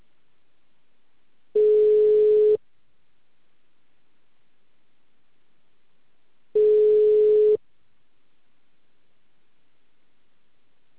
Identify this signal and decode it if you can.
Ringback tone (Europe, 425 Hz).
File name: out- -2044- - - .2917